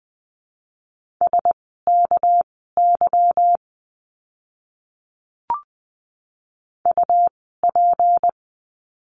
Morse code, heard as a keyed tone, about 19 wpm, spaced as for 14 wpm, 700 Hz. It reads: SKY UP
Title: 199 Lesson 32f - Letter k - Words 20wpm